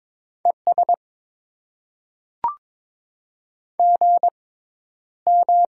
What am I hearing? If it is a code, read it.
Morse code, heard as a keyed tone, about 22 wpm, spaced as for 11 wpm, 700 Hz. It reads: ES GM
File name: Top 90 QSO Elements - 6x Character Spacing - 22wpm